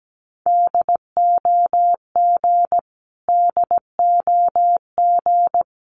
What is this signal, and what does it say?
Morse code, heard as a keyed tone, about 17 wpm, 700 Hz. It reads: DOG DOG
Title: Top 500 Words Repeated 17wpm